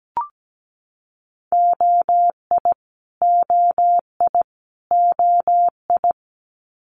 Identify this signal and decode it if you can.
Morse code, heard as a keyed tone, about 17 wpm, 700 Hz. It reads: OI OI OI
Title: Multi-letter Phonemes Repeated 3x 17wpm